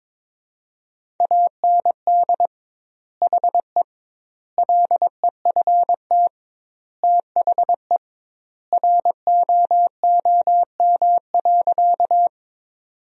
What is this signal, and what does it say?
Morse code, heard as a keyed tone, about 21 wpm, 700 Hz. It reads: AND HE LEFT THE ROOM.
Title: 2x Word-Spacing - Sentences from Top 300 Words - 22wpm